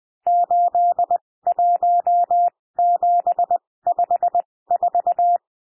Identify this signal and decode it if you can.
Morse code, heard as a keyed tone, about 20 wpm, spaced as for 16 wpm, 700 Hz. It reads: 81754
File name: cwmp3 / N1 SOUND NUMBERS 20 wpm